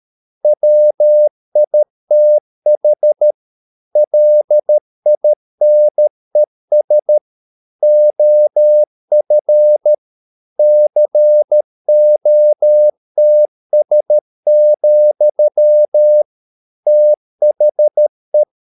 Morse code, heard as a keyed tone, about 13 wpm, 600 Hz. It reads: WITH LINES OF COTS, THE